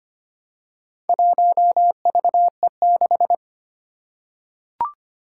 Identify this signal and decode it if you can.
Morse code, heard as a keyed tone, about 25 wpm, 700 Hz. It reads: 1VE6
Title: Four Challenging Letter-Number Combinations 25wpm